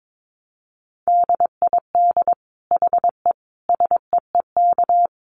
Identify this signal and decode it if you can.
Morse code, heard as a keyed tone, about 22 wpm, 700 Hz. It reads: DID HE SEEK
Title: Dit Challenge 22wpm